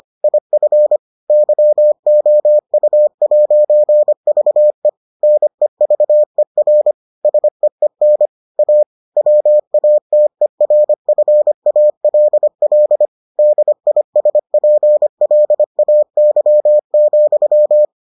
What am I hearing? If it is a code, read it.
Morse code, heard as a keyed tone, about 25 wpm, 600 Hz. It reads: IF YOU'VE NEVER SEEN A WATERFALL DISPLAY,